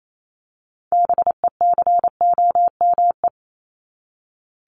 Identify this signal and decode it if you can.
Morse code, heard as a keyed tone, about 28 wpm, 700 Hz. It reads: BECOME
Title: Top 501-750 Words 28wpm